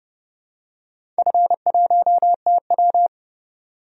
Morse code, heard as a keyed tone, about 31 wpm, 700 Hz. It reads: F1TW
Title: DXCC-100 Call Signs - 4 Characters - 30wpm